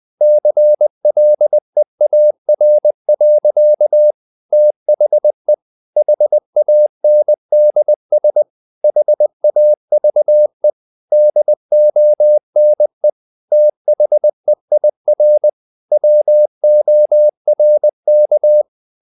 Morse code, heard as a keyed tone, about 20 wpm, 600 Hz. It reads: CLEAR. THE HANDS HAVE DONE THEIR WORK
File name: the old man and the sea part 2 20wpm